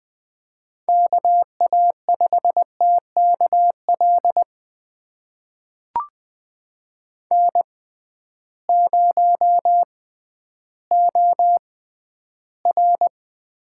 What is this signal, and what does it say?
Morse code, heard as a keyed tone, about 20 wpm, spaced as for 12 wpm, 700 Hz. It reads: KA5TKL N0OR
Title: USA Call Signs - 6x Character Spacing - 20wpm